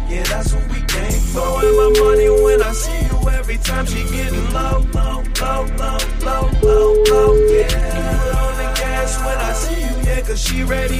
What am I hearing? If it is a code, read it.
Ringback tone (Europe, 425 Hz), with noise.